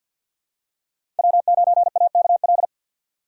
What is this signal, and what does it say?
Morse code, heard as a keyed tone, about 47 wpm, 700 Hz. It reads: W9AKL
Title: USA Call Signs - 5 Characters - 2x Character Spacing - 50wpm